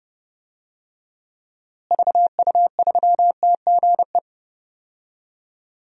Morse code, heard as a keyed tone, about 30 wpm, 700 Hz. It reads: VU3TGE
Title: DXCC-100 Call Signs - 6 Characters - Repeated 2x - 30wpm